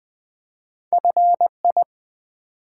Morse code, heard as a keyed tone, about 20 wpm, 700 Hz. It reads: FI
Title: Build A Word - 1 Letter At A Time - 5-Letter Words - 20wpm